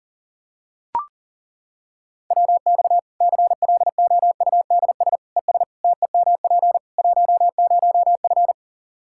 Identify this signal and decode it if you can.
Morse code, heard as a keyed tone, about 40 wpm, 700 Hz. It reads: WX CLOUDS ES TEMP 10F
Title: Practice QSO - Exchange 2 - Info - 40wpm